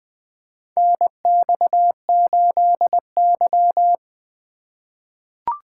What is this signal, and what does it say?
Morse code, heard as a keyed tone, about 20 wpm, 700 Hz. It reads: NX8Y
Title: USA Call Signs - 4 Characters - 6x Character Spacing - 20wpm